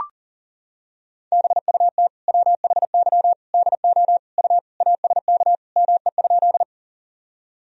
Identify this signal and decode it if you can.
Morse code, heard as a keyed tone, about 40 wpm, 700 Hz. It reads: BUT WHY DO U ASK ME?
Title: Sentences from Top 200 Words and Abbreviations 40wpm